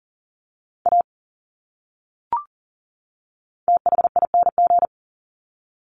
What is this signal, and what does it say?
Morse code, heard as a keyed tone, about 40 wpm, 700 Hz. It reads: A THING
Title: Top 200 Words 40wpm